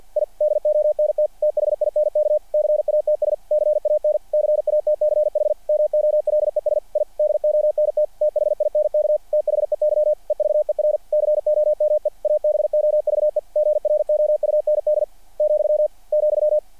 Morse code, heard as a keyed tone, about 48 wpm, 600 Hz. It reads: I DONT THINK CATS CAN CATCH MOLES I DONT THINK THEY EVER COME ABOVE GROUND , ,
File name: archived / CRSnet